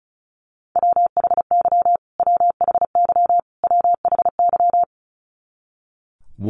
Morse code, heard as a keyed tone, about 34 wpm, 700 Hz. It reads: WHY WHY WHY
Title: Top 200 Words Repeated 35wpm